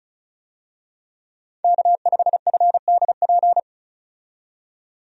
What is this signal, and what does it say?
Morse code, heard as a keyed tone, about 34 wpm, 700 Hz. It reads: K5FDP
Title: USA Call Signs - 5 Characters - 35wpm